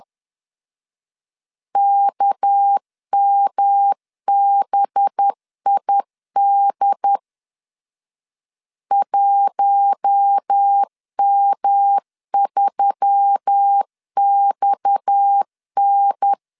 Morse code, heard as a keyed tone, about 10 wpm, 800 Hz. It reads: KMBID 1M3XN
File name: LEZIONI / 8 YJ45 / 8 YJ 10wpm 2651